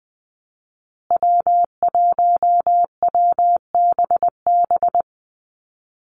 Morse code, heard as a keyed tone, about 20 wpm, 700 Hz. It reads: W1WBB
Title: Call Signs Repeated 2x - 20wpm